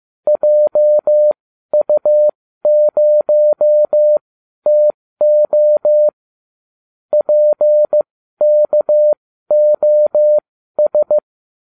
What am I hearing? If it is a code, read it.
Morse code, heard as a keyed tone, about 15 wpm, spaced as for 12 wpm, 600 Hz. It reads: JU0TO PKOS